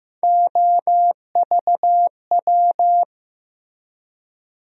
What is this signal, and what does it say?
Morse code, heard as a keyed tone, about 15 wpm, 700 Hz. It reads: OVW
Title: USA Call Sign Postfixes - 3 Letters 15wpm